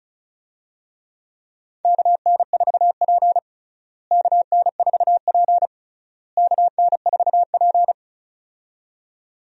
Morse code, heard as a keyed tone, about 36 wpm, 700 Hz. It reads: KN4P KN4P KN4P
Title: USA Call Signs - 4 Characters - Repeated 3x - 35wpm